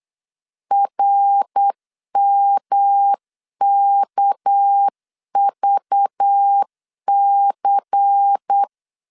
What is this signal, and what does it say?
Morse code, heard as a keyed tone, about 8 wpm, 800 Hz. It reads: RMKVC
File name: LEZIONI / 7 GF9Z / 7 G 8wpm 3118